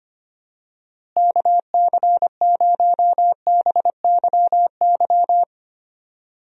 Morse code, heard as a keyed tone, about 25 wpm, 700 Hz. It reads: KC0BYY
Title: USA Call Signs - 6 Characters - Repeated 2x - 25wpm